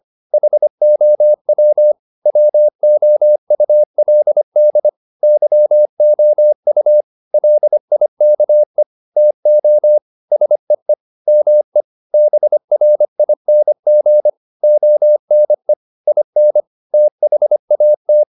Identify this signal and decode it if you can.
Morse code, heard as a keyed tone, about 25 wpm, 600 Hz. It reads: HOW WOULD YOU LIKE TO SEE ME BRING ONE IN THAT
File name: the old man and the sea part 1 25wpm